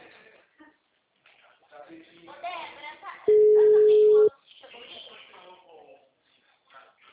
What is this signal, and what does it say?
Ringback tone (Europe, 425 Hz), with speech.